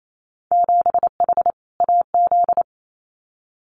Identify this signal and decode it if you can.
Morse code, heard as a keyed tone, about 28 wpm, 700 Hz. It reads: 7H AZ
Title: Winter Field Day Exchanges Encore 28wpm